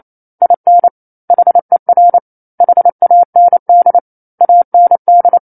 Morse code, heard as a keyed tone, about 28 wpm, 700 Hz. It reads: IN HER HAND AND
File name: 28 WPM